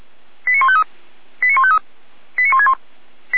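Five-tone selective call (ZVEI), heard as quick runs of five tones, tones of about 70 ms. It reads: 89153 89153 89161
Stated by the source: Five-tone ZVEI selective call received by SDR, fire brigade Feuerwehr Orlamünde.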